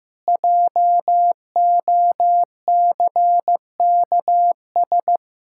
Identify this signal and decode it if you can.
Morse code, heard as a keyed tone, about 15 wpm, 700 Hz. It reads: JOCKS
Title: Pangram Challenge v2 - 15wpm